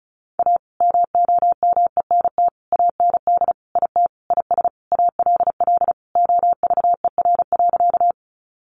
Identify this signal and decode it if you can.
Morse code, heard as a keyed tone, about 34 wpm, 700 Hz. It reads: A MOMENT AND IT IS ALL OVER.
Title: Sentences from Top 1000 Words 35wpm